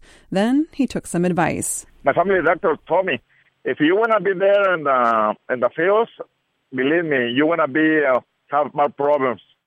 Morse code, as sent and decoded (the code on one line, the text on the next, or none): none